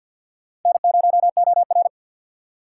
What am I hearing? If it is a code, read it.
Morse code, heard as a keyed tone, about 47 wpm, 700 Hz. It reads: N0OR
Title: USA Call Signs - 4 Characters - 6x Character Spacing - 50wpm